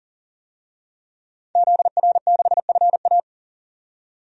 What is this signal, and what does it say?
Morse code, heard as a keyed tone, about 40 wpm, 700 Hz. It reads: ZR6FA
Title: DXCC-100 Call Signs - 5 Characters - 40wpm